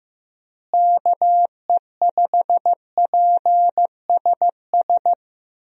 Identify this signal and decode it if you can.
Morse code, heard as a keyed tone, about 15 wpm, 700 Hz. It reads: KE5PSS
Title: USA Call Signs - 6 Characters - Repeated 3x - 15wpm